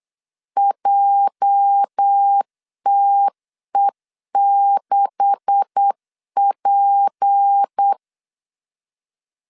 Morse code, heard as a keyed tone, about 8 wpm, 800 Hz. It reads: JTE6P
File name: LEZIONI / 9 286 8wpm 3425